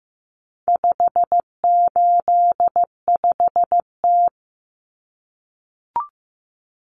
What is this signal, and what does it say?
Morse code, heard as a keyed tone, about 15 wpm, 700 Hz. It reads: 585T
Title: Common Cut Numbers - 4-Digit 15wpm